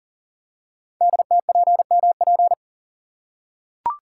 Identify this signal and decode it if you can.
Morse code, heard as a keyed tone, about 42 wpm, 700 Hz. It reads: DTPMP
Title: Code Groups - 5 Letters - 40wpm